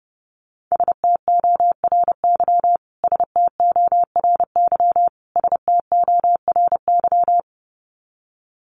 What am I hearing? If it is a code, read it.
Morse code, heard as a keyed tone, about 30 wpm, 700 Hz. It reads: STORY STORY STORY
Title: Top 500 Words Repeated 30wpm